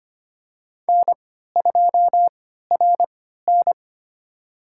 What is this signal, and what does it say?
Morse code, heard as a keyed tone, about 25 wpm, spaced as for 14 wpm, 700 Hz. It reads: N2RN
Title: USA Call Signs - 4 Characters - 3x Character Spacing - 25wpm